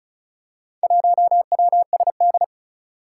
Morse code, heard as a keyed tone, about 36 wpm, 700 Hz. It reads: 1WSD
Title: Four Challenging Letter-Number Combinations 35wpm